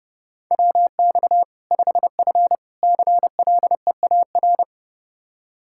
Morse code, heard as a keyed tone, about 30 wpm, 700 Hz. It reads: WX 5F CLEAR
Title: Practice QSO - Exchange 2 - Info - 30wpm